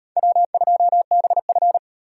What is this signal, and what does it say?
Morse code, heard as a keyed tone, about 36 wpm, 700 Hz. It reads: W2BF